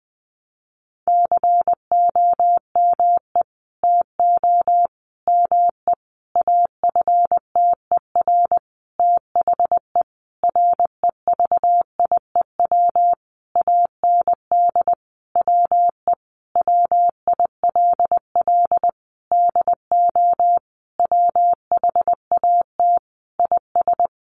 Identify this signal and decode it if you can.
Morse code, heard as a keyed tone, about 20 wpm, 700 Hz. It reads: COME TO ME AFTER THE REVIEW AND WE WILL DO WHAT IS
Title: Sentences from Top 900 Words 20wpm